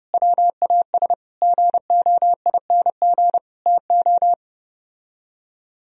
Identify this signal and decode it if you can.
Morse code, heard as a keyed tone, about 31 wpm, 700 Hz. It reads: WAS GOING TO